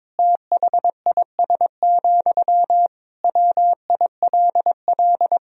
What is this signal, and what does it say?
Morse code, heard as a keyed tone, about 22 wpm, 700 Hz. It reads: THIS, WILL
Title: Sentences from Top 400 Words and Abbreviations 22wpm